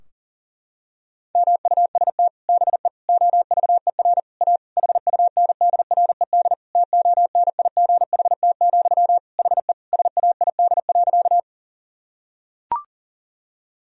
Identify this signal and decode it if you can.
Morse code, heard as a keyed tone, about 37 wpm, 700 Hz. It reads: MUST BE OVER A HUNDRED TONIGHT, HE SAID.
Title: Sentences from Top 1000 Words - Review Part 2 - 40wpm